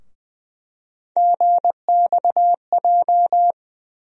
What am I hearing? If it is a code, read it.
Morse code, heard as a keyed tone, about 20 wpm, 700 Hz. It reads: GXJ
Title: USA Call Sign Postfixes - 3 Letters 20wpm